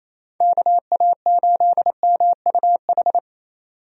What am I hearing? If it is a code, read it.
Morse code, heard as a keyed tone, about 28 wpm, 700 Hz. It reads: KA8MUH
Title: USA Call Signs - 6 Characters - Repeated 3x - 28wpm